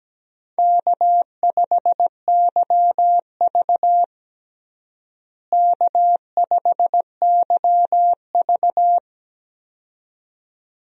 Morse code, heard as a keyed tone, about 17 wpm, 700 Hz. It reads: K5YV K5YV
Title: USA Call Signs - 4 Characters - Repeated 2x - 17wpm